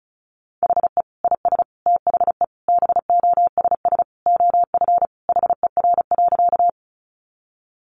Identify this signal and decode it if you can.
Morse code, heard as a keyed tone, about 34 wpm, 700 Hz. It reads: HE IS THE BOSS OF HER.
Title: Sentences from Top 700 Words 35wpm